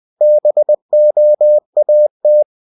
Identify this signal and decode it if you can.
Morse code, heard as a keyed tone, about 20 wpm, 600 Hz. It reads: BOAT
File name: cw-books / winnie-the-pooh / winnie-20wpm-page-0100